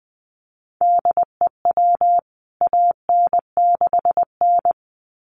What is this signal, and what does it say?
Morse code, heard as a keyed tone, about 20 wpm, 700 Hz. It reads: DEW AN6N
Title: CWT Exchange 20wpm